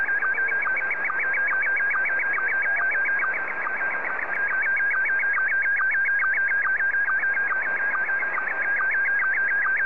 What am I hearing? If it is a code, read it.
SSTV (slow-scan television), Scottie 1.